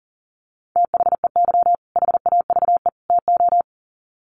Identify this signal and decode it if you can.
Morse code, heard as a keyed tone, about 40 wpm, 700 Hz. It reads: THEY HAVE TO
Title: Sets of 3 Words 40wpm